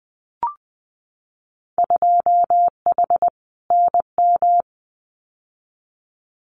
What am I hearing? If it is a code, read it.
Morse code, heard as a keyed tone, about 20 wpm, 700 Hz. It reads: 2H NM
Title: Winter Field Day Exchanges Encore 20wpm